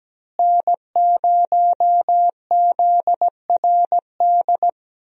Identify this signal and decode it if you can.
Morse code, heard as a keyed tone, about 17 wpm, 700 Hz. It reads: N0ZRD
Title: USA Call Signs - 5 Characters - Repeated 2x - 17wpm